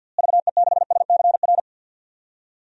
Morse code, heard as a keyed tone, about 46 wpm, 700 Hz. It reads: VE6SCR
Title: DXCC-100 Call Signs - 6 Characters - Repeated 3x - 50wpm